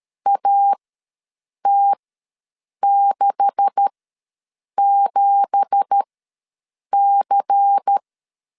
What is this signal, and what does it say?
Morse code, heard as a keyed tone, about 13 wpm, spaced as for 7 wpm, 800 Hz. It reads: AT67C
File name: LEZIONI / 9 2867 ws 12wpm 3431